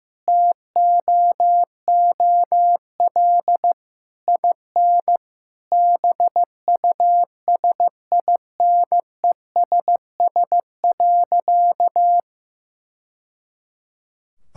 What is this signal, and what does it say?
Morse code, heard as a keyed tone, about 15 wpm, 700 Hz. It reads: TOOL IN BUSINESS.
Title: Sentences from Top 1000 Words - Review Part 6 - 15wpm